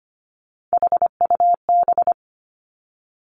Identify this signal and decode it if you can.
Morse code, heard as a keyed tone, about 25 wpm, 700 Hz. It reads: HUB
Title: Three Challenging Letter-Number Combinations 25wpm